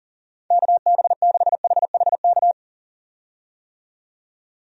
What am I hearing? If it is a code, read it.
Morse code, heard as a keyed tone, about 40 wpm, 690 Hz. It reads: KB6HHK
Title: USA Call Signs - 6 Characters - 40wpm